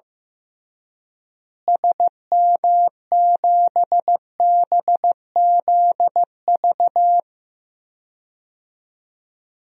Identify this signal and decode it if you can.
Morse code, heard as a keyed tone, about 15 wpm, 700 Hz. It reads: SM7BZV
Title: DXCC-100 Call Signs - 6 Characters - Repeated 3x - 15wpm